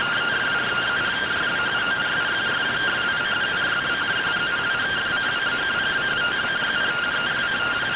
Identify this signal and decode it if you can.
Two-tone teleprinter signal, about 170 Hz shift.